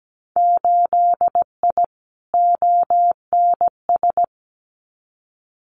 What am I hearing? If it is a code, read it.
Morse code, heard as a keyed tone, about 17 wpm, 700 Hz. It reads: 8I ONS
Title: Winter Field Day Exchanges Encore 17wpm